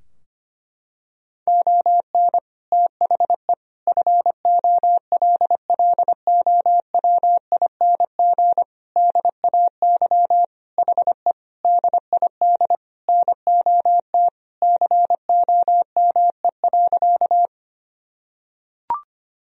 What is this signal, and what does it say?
Morse code, heard as a keyed tone, about 25 wpm, 700 Hz. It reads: ON THE FOLLOWING DAY HE DID NOT COME.
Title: Sentences from Top 1000 Words - Review Part 4 - 25wpm